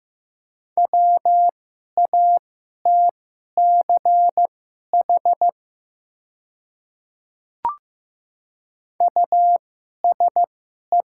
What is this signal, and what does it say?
Morse code, heard as a keyed tone, about 15 wpm, spaced as for 11 wpm, 690 Hz. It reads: WATCH USE
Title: 2x Character Spacing - Top 500 Words - 15wpm